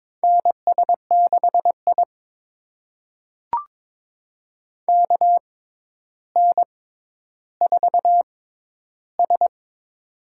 Morse code, heard as a keyed tone, about 22 wpm, spaced as for 11 wpm, 700 Hz. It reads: NS6I KN4S
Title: USA Call Signs - 4 Characters - 6x Character Spacing - 22wpm